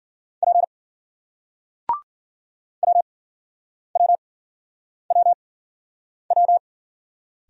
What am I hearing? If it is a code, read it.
Morse code, heard as a keyed tone, about 55 wpm, spaced as for 9 wpm, 700 Hz. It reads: P WWWW